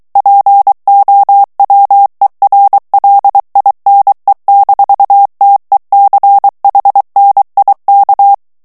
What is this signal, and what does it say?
Morse code, heard as a keyed tone, about 23 wpm, 800 Hz. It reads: POWERLINE-TECHNIK
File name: morse mp3 / ALLEMAND texte clair 1 vitesse 22 mots minute